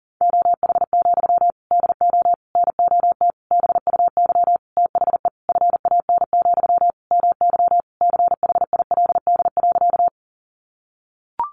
Morse code, heard as a keyed tone, about 40 wpm, 700 Hz. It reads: OH, DO NOT BUY THE FAN, MY CHILD.
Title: Sentences from Top 1300 Words 40wpm